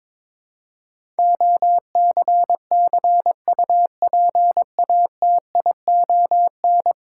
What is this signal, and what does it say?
Morse code, heard as a keyed tone, about 21 wpm, 700 Hz. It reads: OCCUPATION
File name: practice / 22wpm / 10 Letter Words - First 3 Letters Given - 22wpm